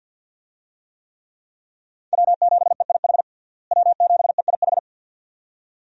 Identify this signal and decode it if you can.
Morse code, heard as a keyed tone, about 51 wpm, 700 Hz. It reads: W7EIH W7EIH
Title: USA Call Signs - 5 Characters - Repeated 2x - 50wpm